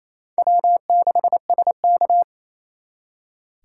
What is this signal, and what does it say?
Morse code, heard as a keyed tone, about 35 wpm, 700 Hz. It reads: W6SK
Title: Speed Racing - USA 1x2 Call Signs 28wpm